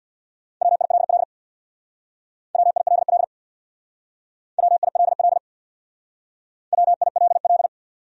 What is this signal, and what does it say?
Morse code, heard as a keyed tone, about 65 wpm, 700 Hz. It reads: WILL WILL WILL WILL